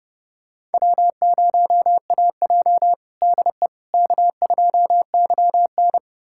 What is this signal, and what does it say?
Morse code, heard as a keyed tone, about 30 wpm, 700 Hz. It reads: W0AJ DE K2YN
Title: USA Callsign DE Callsign - 4 Characters - 30wpm